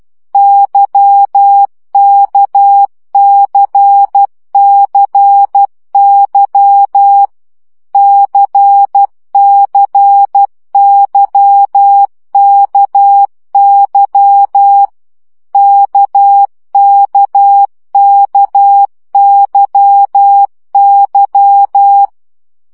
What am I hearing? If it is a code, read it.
Morse code, heard as a keyed tone, about 12 wpm, 800 Hz. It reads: YKCCY CCYKY KKKYY
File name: CW gyakorlas k c y 10 baud